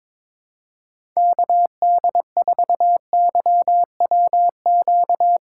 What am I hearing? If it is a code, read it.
Morse code, heard as a keyed tone, about 21 wpm, 700 Hz. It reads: KD4YWQ
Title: USA Call Signs - 6 Characters - 3x Character Spacing - 22wpm